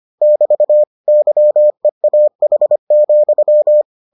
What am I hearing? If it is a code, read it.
Morse code, heard as a keyed tone, about 25 wpm, 600 Hz. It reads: <BT> YEAH,